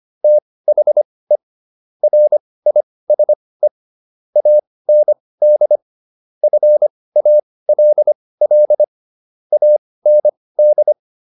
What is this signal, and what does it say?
Morse code, heard as a keyed tone, about 25 wpm, spaced as for 18 wpm, 600 Hz. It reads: THE RISE AND FALL AND